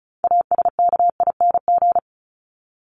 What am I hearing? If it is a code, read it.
Morse code, heard as a keyed tone, about 34 wpm, 700 Hz. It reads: ASKING
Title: ING Suffix - Top 100 Words with Base Words 35wpm